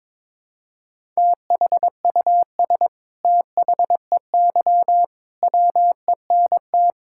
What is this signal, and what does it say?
Morse code, heard as a keyed tone, about 22 wpm, 700 Hz. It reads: THUS THEY WENT
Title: Sentences from Top 1000 Words - Review Part 5 - 22wpm